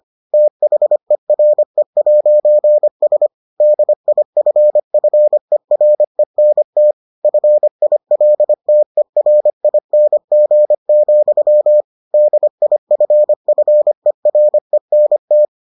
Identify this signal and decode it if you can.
Morse code, heard as a keyed tone, about 25 wpm, 600 Hz. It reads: THERE'S DIFFERENT FILTERING, DIFFERENT